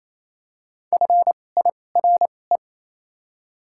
Morse code, heard as a keyed tone, about 28 wpm, spaced as for 20 wpm, 700 Hz. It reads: FIRE
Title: 2x Character Spacing - Top 500 Words - 28wpm